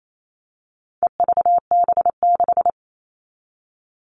Morse code, heard as a keyed tone, about 28 wpm, 700 Hz. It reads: EVB6
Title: All Cut Numbers - 4-Digit 28wpm